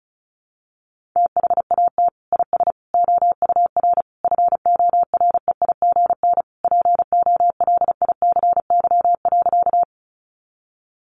Morse code, heard as a keyed tone, about 34 wpm, 700 Hz. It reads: THAT IS OUR FOREIGN POLICY.
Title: Sentences from Top 1400 Words 35wpm